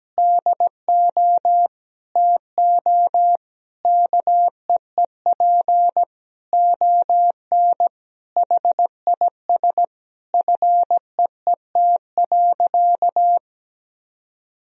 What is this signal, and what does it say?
Morse code, heard as a keyed tone, about 17 wpm, 700 Hz. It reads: DO TO KEEP ON HIS FEET.
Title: Sentences from Top 1000 Words - Review Part 1 - 17wpm